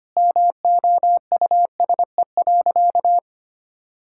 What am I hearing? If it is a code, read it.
Morse code, heard as a keyed tone, about 25 wpm, 700 Hz. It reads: MOUSE.
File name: practice / 25wpm / All-Age Appropriate Jokes - 25wpm